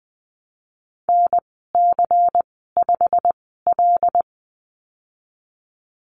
Morse code, heard as a keyed tone, about 20 wpm, spaced as for 15 wpm, 700 Hz. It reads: NC5L
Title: 154 Lesson 25g - Letter p - Call Signs 20wpm